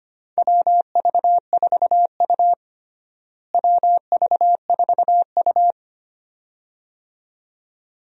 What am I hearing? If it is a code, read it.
Morse code, heard as a keyed tone, about 25 wpm, 700 Hz. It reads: WV4U WV4U
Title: USA Call Signs - 4 Characters - Repeated 2x - 25wpm